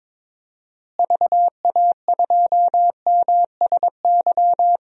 Morse code, heard as a keyed tone, about 22 wpm, 700 Hz. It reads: VA2MSY
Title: DXCC-100 Call Signs - 6 Characters - Repeated 3x - 22wpm